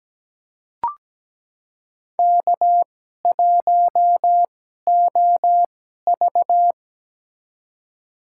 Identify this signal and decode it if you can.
Morse code, heard as a keyed tone, about 17 wpm, spaced as for 12 wpm, 700 Hz. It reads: K1OV
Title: USA Call Signs - 4 Characters - 2x Character Spacing - 17wpm